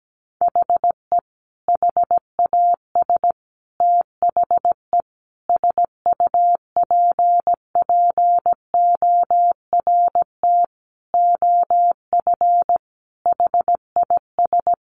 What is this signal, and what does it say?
Morse code, heard as a keyed tone, about 17 wpm, 700 Hz. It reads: HE HAS THE SUPPORT OF HIS
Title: Sentences from Top 800 Words 17wpm